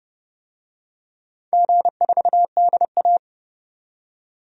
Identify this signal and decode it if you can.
Morse code, heard as a keyed tone, about 29 wpm, 700 Hz. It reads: G4DA
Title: DXCC-100 Call Signs - 4 Characters - 30wpm